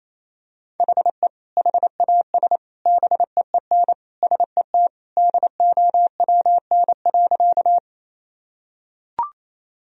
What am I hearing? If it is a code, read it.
Morse code, heard as a keyed tone, about 28 wpm, 700 Hz. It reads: HE HAS BEEN SET DOWN.
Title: Sentences from Top 1000 Words - Review Part 1 - 28wpm